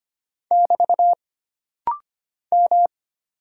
Morse code, heard as a keyed tone, about 25 wpm, 700 Hz. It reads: <BT> M
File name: Single Character (Letter:Number:Punctuation:Prosign) - Rapid-Fire - 25wpm